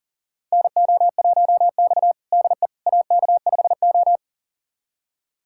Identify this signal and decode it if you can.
Morse code, heard as a keyed tone, about 40 wpm, 700 Hz. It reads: NO1X DE AK5O
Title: USA Callsign DE Callsign - 4 Characters - 40wpm